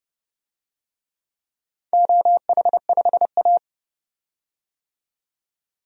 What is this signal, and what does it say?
Morse code, heard as a keyed tone, about 32 wpm, 700 Hz. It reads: OH5A
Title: DXCC-100 Call Signs - 4 Characters - 30wpm